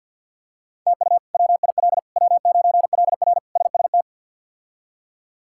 Morse code, heard as a keyed tone, about 51 wpm, 700 Hz. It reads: TU WIL W9LR SST
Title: K1USN SST QSO Practice - 50wpm